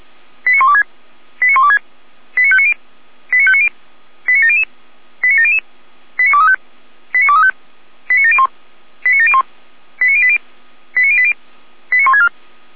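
Five-tone selective call (ZVEI), heard as quick runs of five tones, tones of about 80 ms. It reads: 89126 89126 89590 89590 89700 89700 89235 89235 89701 89701 89080 89080 89164
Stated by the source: Five-tone ZVEI selective call 89115, received by SDR.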